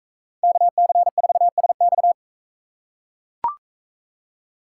Morse code, heard as a keyed tone, about 42 wpm, 700 Hz. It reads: KC4SX